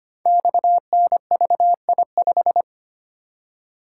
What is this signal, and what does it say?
Morse code, heard as a keyed tone, about 25 wpm, 700 Hz. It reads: XNVI5
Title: Code Groups - 5 Letter:Numbers - 25wpm